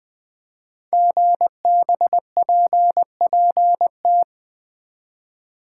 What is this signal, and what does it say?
Morse code, heard as a keyed tone, about 20 wpm, 700 Hz. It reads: GBPPT
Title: Code Groups - 5 Letters - 20wpm